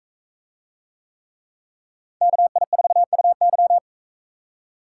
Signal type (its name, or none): Morse code